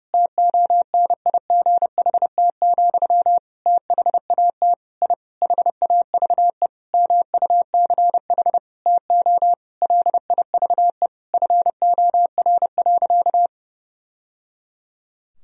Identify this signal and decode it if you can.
Morse code, heard as a keyed tone, about 32 wpm, 700 Hz. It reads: TONIGHT, THAT I HAVE MUCH TO LIVE FOR.
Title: Sentences from Top 1000 Words - Review Part 3 - 30wpm